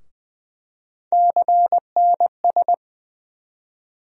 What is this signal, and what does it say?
Morse code, heard as a keyed tone, about 20 wpm, 700 Hz. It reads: CNS
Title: USA Call Sign Postfixes - 3 Letters 20wpm